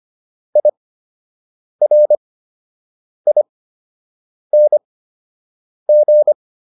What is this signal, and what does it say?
Morse code, heard as a keyed tone, about 25 wpm, spaced as for 7 wpm, 600 Hz. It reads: IRING